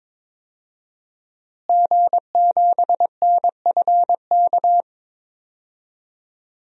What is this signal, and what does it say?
Morse code, heard as a keyed tone, about 22 wpm, 700 Hz. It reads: G7NFK
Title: DXCC-100 Call Signs - 5 Characters - Repeated 3x - 22wpm